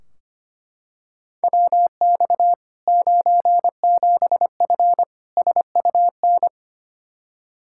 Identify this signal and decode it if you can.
Morse code, heard as a keyed tone, about 25 wpm, 700 Hz. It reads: WX 97F SUN
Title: Practice QSO - Exchange 2 - Info - 25wpm